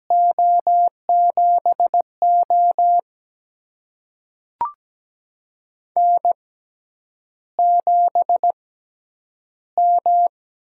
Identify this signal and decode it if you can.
Morse code, heard as a keyed tone, about 17 wpm, spaced as for 9 wpm, 700 Hz. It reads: O7O N7M
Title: USA Call Signs - 4 Characters - 6x Character Spacing - 17wpm